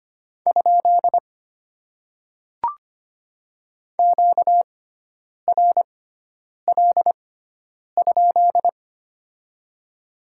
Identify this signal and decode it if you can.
Morse code, heard as a keyed tone, about 25 wpm, spaced as for 9 wpm, 690 Hz. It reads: ? QRL?
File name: Top 90 QSO Elements - 6x Character Spacing - 25wpm